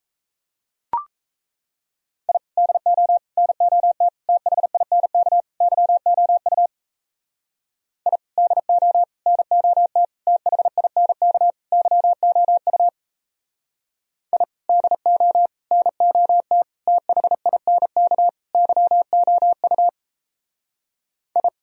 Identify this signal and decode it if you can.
Morse code, heard as a keyed tone, about 34 wpm, 700 Hz. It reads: I DO NOT THINK YOU I DO NOT THINK YOU I DO NOT THINK YOU I
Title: Speed-Racing - Sets of 5 Words 28wpm